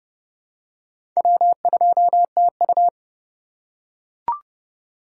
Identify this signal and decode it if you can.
Morse code, heard as a keyed tone, about 30 wpm, 700 Hz. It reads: W2TU
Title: USA Call Signs - 4 Characters - 3x Character Spacing - 30wpm